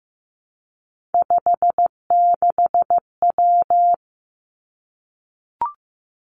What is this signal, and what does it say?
Morse code, heard as a keyed tone, about 15 wpm, 700 Hz. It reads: 56W
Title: Three Challenging Letter-Number Combinations 15wpm